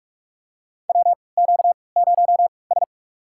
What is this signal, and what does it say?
Morse code, heard as a keyed tone, about 47 wpm, spaced as for 26 wpm, 700 Hz. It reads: WQ0S
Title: USA Call Signs - 4 Characters - 3x Character Spacing - 45wpm